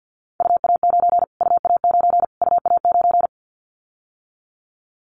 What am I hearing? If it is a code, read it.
Morse code, heard as a keyed tone, about 50 wpm, 700 Hz. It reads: UA9 UA9 UA9
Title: Callsign Prefixes for Top 25 DXCC - Repeated 3x - 50wpm